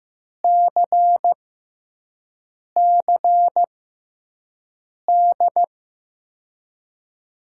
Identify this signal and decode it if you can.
Morse code, heard as a keyed tone, about 15 wpm, spaced as for 5 wpm, 700 Hz. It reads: CCD
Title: USA Call Signs - 5 Characters - 6x Character Spacing - 15wpm